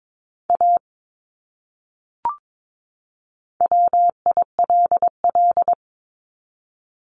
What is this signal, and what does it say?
Morse code, heard as a keyed tone, about 22 wpm, 700 Hz. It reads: A WILL
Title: Top 200 Words 22wpm